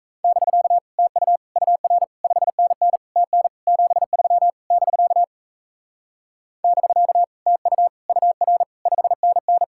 Morse code, heard as a keyed tone, about 37 wpm, 700 Hz. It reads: <BK> TU UR 5NN TN 73 <BK> <BK> TU UR 5NN